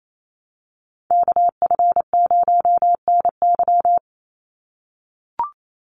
Morse code, heard as a keyed tone, about 28 wpm, 700 Hz. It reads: KF0NY
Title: USA Call Signs 28wpm